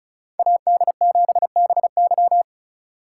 Morse code, heard as a keyed tone, about 36 wpm, 700 Hz. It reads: AD7BY
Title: USA Call Signs - 5 Characters - 6x Character Spacing - 35wpm